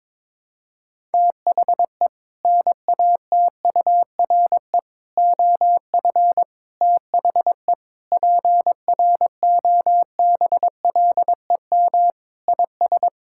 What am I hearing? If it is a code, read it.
Morse code, heard as a keyed tone, about 22 wpm, 700 Hz. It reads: THE NATURE OF THE PROBLEM IS
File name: practice / 22wpm / Sentences with 3-Word Phrases - 22wpm